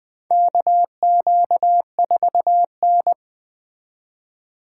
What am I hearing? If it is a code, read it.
Morse code, heard as a keyed tone, about 20 wpm, 700 Hz. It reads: KQ4N
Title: USA Call Signs - 4 Characters - 3x Character Spacing - 20wpm